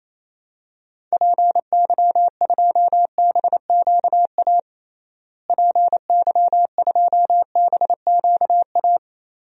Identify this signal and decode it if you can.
Morse code, heard as a keyed tone, about 28 wpm, 700 Hz. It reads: PY2BQA PY2BQA
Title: DXCC-100 Call Signs - 6 Characters - Repeated 3x - 28wpm